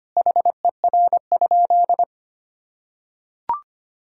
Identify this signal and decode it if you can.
Morse code, heard as a keyed tone, about 25 wpm, 700 Hz. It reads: HER?
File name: practice / 25wpm / Sentences with 2-Word Phrases - 25wpm